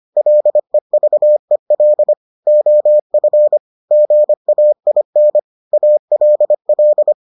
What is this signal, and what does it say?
Morse code, heard as a keyed tone, about 25 wpm, 600 Hz. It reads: LEVEL OF GAIN ALL